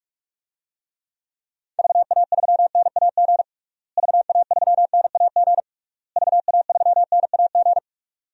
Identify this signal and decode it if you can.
Morse code, heard as a keyed tone, about 46 wpm, 700 Hz. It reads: VA3NAG VA3NAG VA3NAG
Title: DXCC-100 Call Signs - 6 Characters - Repeated 3x - 45wpm